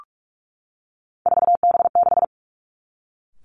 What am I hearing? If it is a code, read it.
Morse code, heard as a keyed tone, about 46 wpm, 700 Hz. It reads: 4B6
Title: Three Challenging Letter-Number Combinations 45wpm